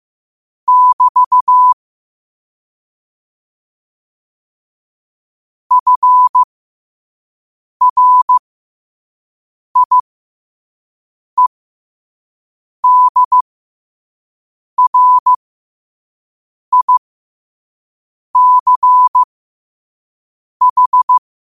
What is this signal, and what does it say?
Morse code, heard as a keyed tone, about 15 wpm, spaced as for 5 wpm, 1000 Hz. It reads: <BT> FRIEDRICH
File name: qotdfiles / 5wpm